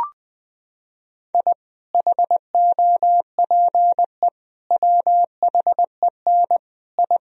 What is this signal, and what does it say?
Morse code, heard as a keyed tone, about 20 wpm, 700 Hz. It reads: I HOPE WHEN I
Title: Sentences from Top 1000 Words - Review Part 1 - 20wpm